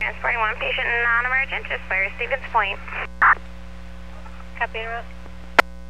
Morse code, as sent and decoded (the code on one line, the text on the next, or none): none